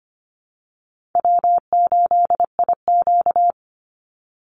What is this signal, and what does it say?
Morse code, heard as a keyed tone, about 25 wpm, 700 Hz. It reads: W8IQ
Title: Easy Call Signs Repeated 2x - 25wpm